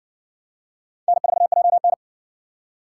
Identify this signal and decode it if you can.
Morse code, heard as a keyed tone, about 58 wpm, 700 Hz. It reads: N4JN